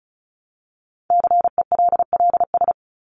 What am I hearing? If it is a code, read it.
Morse code, heard as a keyed tone, about 34 wpm, 700 Hz. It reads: CELLS
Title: ES:S Suffix - Top 100 Words with Base Words 35wpm